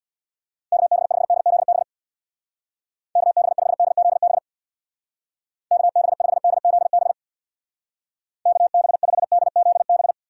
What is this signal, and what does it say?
Morse code, heard as a keyed tone, about 65 wpm, 700 Hz. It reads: KB5DZB KB5DZB KB5DZB KB5DZB